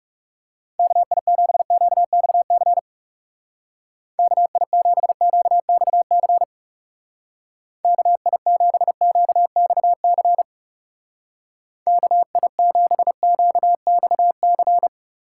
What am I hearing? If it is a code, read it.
Morse code, heard as a keyed tone, about 36 wpm, 700 Hz. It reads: KI7QXC KI7QXC KI7QXC KI7QXC
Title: Speed Racing - USA 2x3 Call Signs 30wpm